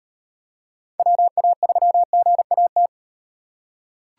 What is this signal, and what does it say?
Morse code, heard as a keyed tone, about 37 wpm, 700 Hz. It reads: WA3GAT